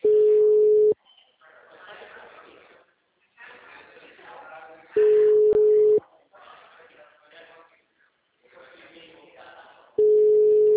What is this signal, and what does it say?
Ringback tone (Europe, 425 Hz), with speech.